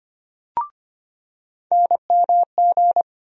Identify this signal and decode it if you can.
Morse code, heard as a keyed tone, about 25 wpm, 700 Hz. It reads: NMG
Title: USA Call Sign Postfixes - 3 Letters 25wpm